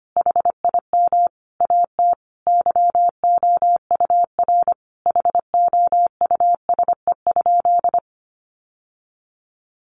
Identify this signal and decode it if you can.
Morse code, heard as a keyed tone, about 25 wpm, 700 Hz. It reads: HIM AT YOUR HOUSE?
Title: Sentences from Top 1000 Words 25wpm